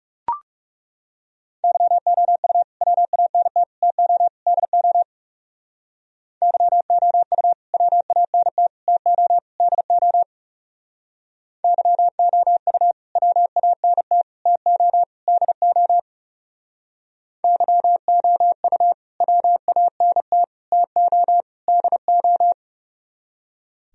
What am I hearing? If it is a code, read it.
Morse code, heard as a keyed tone, about 36 wpm, 700 Hz. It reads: YOU WANT TO DO YOU WANT TO DO YOU WANT TO DO YOU WANT TO DO
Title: Speed-Racing - Sets of 4 Words 30wpm